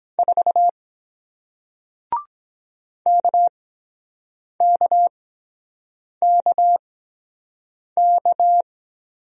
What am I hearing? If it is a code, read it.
Morse code, heard as a keyed tone, about 23 wpm, spaced as for 7 wpm, 700 Hz. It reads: 4 KKKK